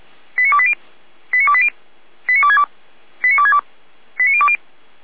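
Five-tone selective call (ZVEI), heard as quick runs of five tones, tones of about 70 ms. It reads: 89280 89280 89262 89262 89020
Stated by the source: ZVEI five-tone selective call received on an SDR.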